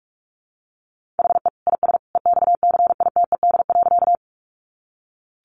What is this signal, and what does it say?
Morse code, heard as a keyed tone, about 44 wpm, 700 Hz. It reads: HE IS EXCITED.
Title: Additional Easy Phrases 45wpm